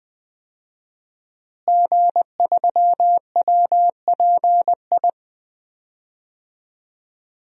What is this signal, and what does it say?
Morse code, heard as a keyed tone, about 20 wpm, 700 Hz. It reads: G3WPI
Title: DXCC-100 Call Signs - 5 Characters - Repeated 2x - 20wpm